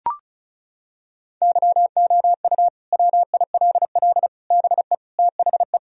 Morse code, heard as a keyed tone, about 36 wpm, 700 Hz. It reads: YOU WILL BE THE
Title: Sentences from Top 1000 Words - Review Part 4 - 35wpm